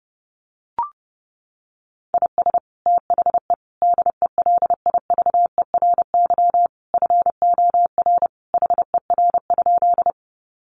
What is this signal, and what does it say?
Morse code, heard as a keyed tone, about 30 wpm, 700 Hz. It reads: IS THE DELIVERY FOR HER?
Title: Sentences from Top 1500 Words 30wpm